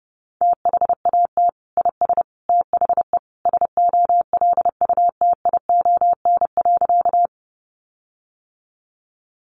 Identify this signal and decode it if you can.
Morse code, heard as a keyed tone, about 30 wpm, 700 Hz. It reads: THAT IS THE SOLUTION.
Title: Sentences from Top 1300 Words 30wpm